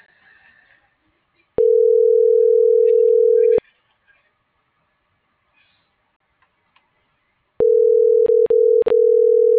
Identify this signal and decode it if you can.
Ringback tone (North America), 440 + 480 Hz.